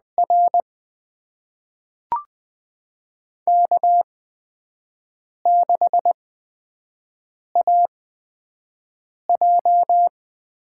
Morse code, heard as a keyed tone, about 20 wpm, spaced as for 5 wpm, 700 Hz. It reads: R K6AJ
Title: USA Call Signs - 4 Characters - 8x Character Spacing - 20wpm